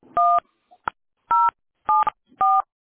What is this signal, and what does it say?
DTMF, heard as telephone touch tones, tones of about 220 ms. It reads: 1 074
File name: baeedbed-5505-439d-a06a